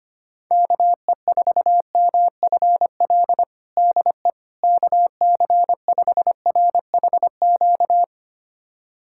Morse code, heard as a keyed tone, about 25 wpm, 700 Hz. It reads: KE4MFL DE KC5RHQ
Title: USA Callsign DE Callsign 25wpm